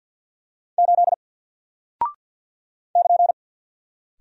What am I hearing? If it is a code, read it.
Morse code, heard as a keyed tone, about 48 wpm, 700 Hz. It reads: 8 <KN>
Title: Single Character (Letter:Number:Punctuation:Prosign) Alt 50wpm